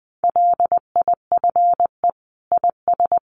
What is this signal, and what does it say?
Morse code, heard as a keyed tone, about 20 wpm, 700 Hz. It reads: LIFE IS
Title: Sentences from Top 700 Words 20wpm